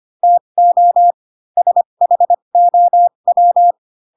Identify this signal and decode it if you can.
Morse code, heard as a keyed tone, about 25 wpm, spaced as for 22 wpm, 700 Hz. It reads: TO SHOW
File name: Pinocchio / Pinocchio ch2 18 25 ik1hge